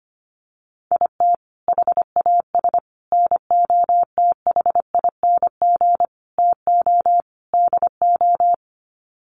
Morse code, heard as a keyed tone, about 25 wpm, 700 Hz. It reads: IT HAS NOTHING TO DO
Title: Sets of 5 Words Encore 25wpm